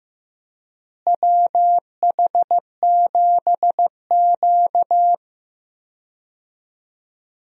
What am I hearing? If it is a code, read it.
Morse code, heard as a keyed tone, about 15 wpm, 700 Hz. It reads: WH7Q
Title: USA Call Signs - 4 Characters - Repeated 3x - 15wpm